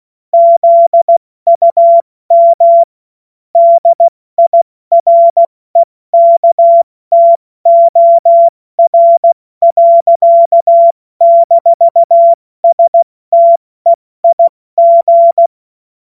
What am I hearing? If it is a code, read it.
Morse code, heard as a keyed tone, about 16 wpm, spaced as for 14 wpm, 680 Hz. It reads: ZUM DIREKTOR. - STEIG
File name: Morsekurs / Witze / Witz 24. August 2021, 70 F80